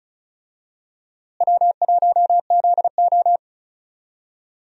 Morse code, heard as a keyed tone, about 36 wpm, 700 Hz. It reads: W1ZO
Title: USA Call Signs - 4 Characters - Repeated 2x - 35wpm